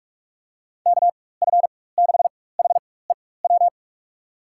Morse code, heard as a keyed tone, about 47 wpm, spaced as for 21 wpm, 700 Hz. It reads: KF6HEW
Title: USA Call Signs - 4x Character Spacing - 45wpm